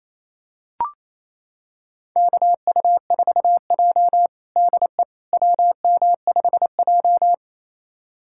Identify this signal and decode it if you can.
Morse code, heard as a keyed tone, about 28 wpm, 700 Hz. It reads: KU4J DE WM5J
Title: USA Callsign DE Callsign - 4 Characters - 28wpm